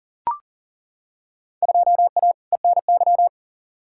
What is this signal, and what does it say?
Morse code, heard as a keyed tone, about 39 wpm, 700 Hz. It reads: 2A ENY
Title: Field Day Exchanges Encore - 40wpm